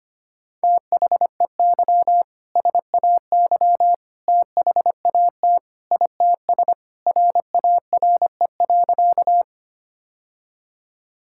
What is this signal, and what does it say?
Morse code, heard as a keyed tone, about 25 wpm, 700 Hz. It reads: THEY SAY THAT ITS RARE.
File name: practice / 25wpm / Sentences with 2-Word Phrases - 25wpm